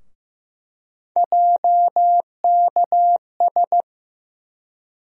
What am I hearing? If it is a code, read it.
Morse code, heard as a keyed tone, about 15 wpm, 700 Hz. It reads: JKS
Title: USA Call Sign Postfixes - 3 Letters 15wpm